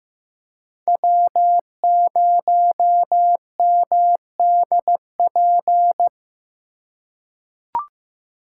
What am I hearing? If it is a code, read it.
Morse code, heard as a keyed tone, about 15 wpm, 700 Hz. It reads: W0MDP
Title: USA Call Signs - 4x Character Spacing - 15wpm